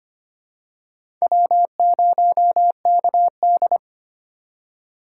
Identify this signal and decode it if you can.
Morse code, heard as a keyed tone, about 25 wpm, 700 Hz. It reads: W0KD
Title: USA Call Signs - 4 Characters - 3x Character Spacing - 25wpm